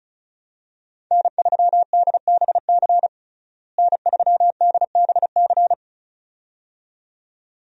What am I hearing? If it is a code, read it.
Morse code, heard as a keyed tone, about 34 wpm, 700 Hz. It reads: N3DBC N3DBC
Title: USA Call Signs - 5 Characters - Repeated 2x - 35wpm